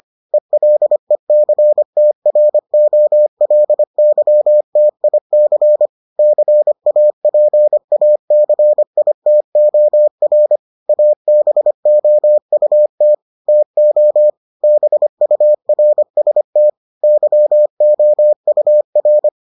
Morse code, heard as a keyed tone, about 25 wpm, 600 Hz. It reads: ELECTROLYTIC CAPACITOR ABOUT TO BURST YOUR